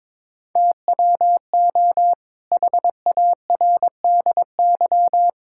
Morse code, heard as a keyed tone, about 22 wpm, 700 Hz. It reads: TWO HARDY
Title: Pangram Challenge v2 - 22wpm